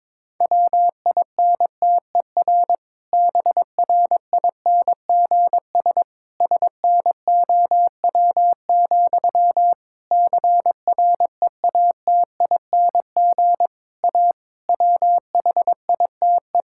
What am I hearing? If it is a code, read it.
Morse code, heard as a keyed tone, about 21 wpm, 700 Hz. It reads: WINTER BRINGS SNOW, CREATING A WHITE
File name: Difficult Sentences with Hints - 22wpm